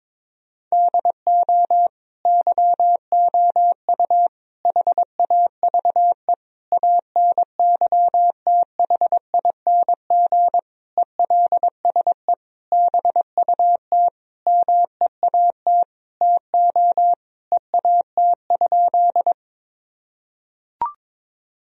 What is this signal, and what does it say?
Morse code, heard as a keyed tone, about 22 wpm, 700 Hz. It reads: DO YOU HAVE ANYTHING ELSE BUT MEAT TO EAT?
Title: Sentences from Top 1000 Words - Review Part 3 - 22wpm